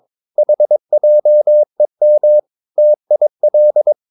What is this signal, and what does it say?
Morse code, heard as a keyed tone, about 22 wpm, 600 Hz. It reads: HJEM TIL
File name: Ferie QRS 22wpm